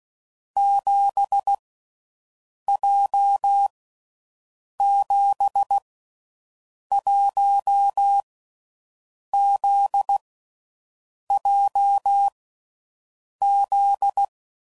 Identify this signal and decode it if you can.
Morse code, heard as a keyed tone, about 16 wpm, 800 Hz. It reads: 7 J 7 1 Z J Z
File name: Groups / Basic Letter Groups / JZ17